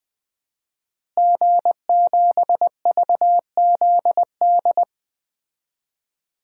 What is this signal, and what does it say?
Morse code, heard as a keyed tone, about 20 wpm, 700 Hz. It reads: G7VZD
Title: DXCC-100 Call Signs - 5 Characters - Repeated 3x - 20wpm